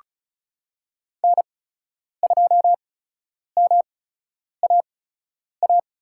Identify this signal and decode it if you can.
Morse code, heard as a keyed tone, about 36 wpm, spaced as for 10 wpm, 700 Hz. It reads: N2MAA
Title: USA Call Signs - 8x Character Spacing - 35wpm